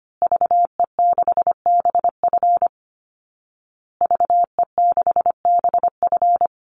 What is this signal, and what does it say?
Morse code, heard as a keyed tone, about 25 wpm, 700 Hz. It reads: VE6BF VE6BF
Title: Call Signs Repeated 2x - 25wpm